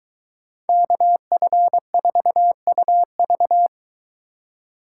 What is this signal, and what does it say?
Morse code, heard as a keyed tone, about 22 wpm, 700 Hz. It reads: KF4UV